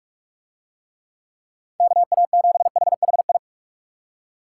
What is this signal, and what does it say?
Morse code, heard as a keyed tone, about 40 wpm, 700 Hz. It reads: KA7HHI